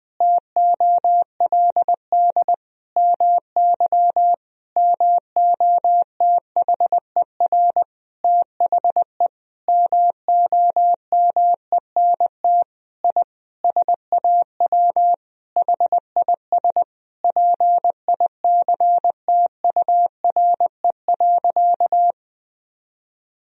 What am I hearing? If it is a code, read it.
Morse code, heard as a keyed tone, about 20 wpm, 700 Hz. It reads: TOLD MY MOTHER THE MOMENT I SAW HIS PICTURE.
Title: Sentences from Top 1000 Words - Review Part 1 - 20wpm